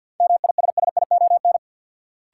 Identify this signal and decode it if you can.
Morse code, heard as a keyed tone, about 49 wpm, 700 Hz. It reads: MISSION
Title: Sentences from Top 1600 Words - New Word Drill Alt - 50wpm